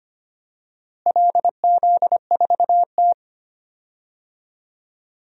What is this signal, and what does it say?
Morse code, heard as a keyed tone, about 25 wpm, 700 Hz. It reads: LZ4T
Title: DXCC-100 Call Signs - 4 Characters - Repeated 2x - 25wpm